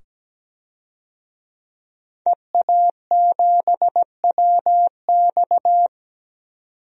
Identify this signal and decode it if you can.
Morse code, heard as a keyed tone, about 17 wpm, 700 Hz. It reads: EA7WX
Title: DXCC-100 Call Signs - 5 Characters - 17wpm